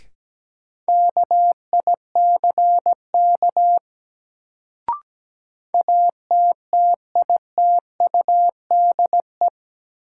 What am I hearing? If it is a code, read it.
Morse code, heard as a keyed tone, about 17 wpm, 700 Hz. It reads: KICK ATTITUDE
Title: Sentences from Top 1300 Words - New Word Drill - 17wpm